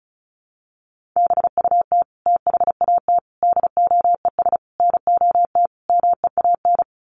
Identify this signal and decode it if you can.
Morse code, heard as a keyed tone, about 34 wpm, 700 Hz. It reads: BUT THAT DOES NOT MEAN
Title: Sets of 5 Words 35wpm